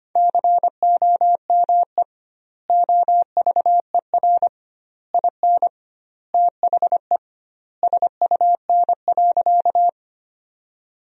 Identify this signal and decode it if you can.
Morse code, heard as a keyed tone, about 25 wpm, 700 Hz. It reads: COME OVER IN THE SUN.
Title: 2x Word-Spacing - Sentences from Top 300 Words - 25wpm